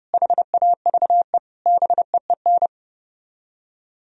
Morse code, heard as a keyed tone, about 37 wpm, 700 Hz. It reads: HAVE BEEN
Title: Speed-Racing - Sets of 4 Words 30wpm